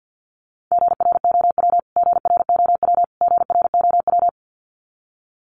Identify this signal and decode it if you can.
Morse code, heard as a keyed tone, about 50 wpm, 700 Hz. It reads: GROW GROW GROW
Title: Top 500 Words Repeated 50wpm